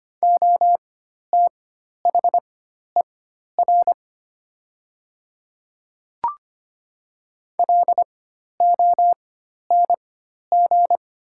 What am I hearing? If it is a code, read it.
Morse code, heard as a keyed tone, about 25 wpm, spaced as for 12 wpm, 700 Hz. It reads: OTHER LONG
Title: 4x Character Spacing - Top 100 Words - 25wpm